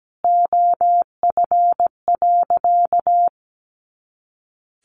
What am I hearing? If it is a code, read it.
Morse code, heard as a keyed tone, about 17 wpm, 700 Hz. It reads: OF.
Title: Easy Phrases 17wpm